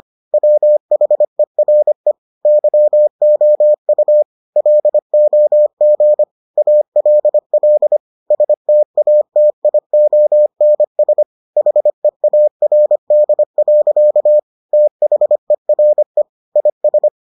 Morse code, heard as a keyed tone, about 25 wpm, 600 Hz. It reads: WHERE YOU LOG ALL STATIONS HEARD. THERE IS